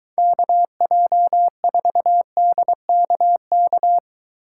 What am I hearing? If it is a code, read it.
Morse code, heard as a keyed tone, about 23 wpm, 700 Hz. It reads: KJ4DKK